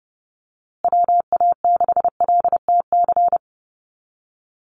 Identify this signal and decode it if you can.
Morse code, heard as a keyed tone, about 30 wpm, 700 Hz. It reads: WA6LTC
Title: USA Call Signs Repeated 3x 30wpm